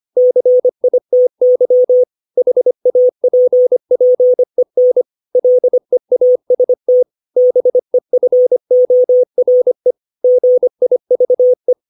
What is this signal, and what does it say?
Morse code, heard as a keyed tone, about 25 wpm, 500 Hz. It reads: CITY HAPPEN LEAST BEFORE GIVE
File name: 632 words / 632 Words 25WPM